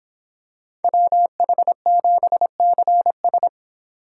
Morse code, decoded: WH7CS

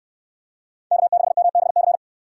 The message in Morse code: K6MBC

